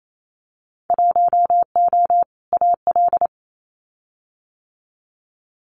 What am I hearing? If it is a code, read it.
Morse code, heard as a keyed tone, about 28 wpm, 700 Hz. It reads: 1O AL
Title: Winter Field Day Exchanges 28wpm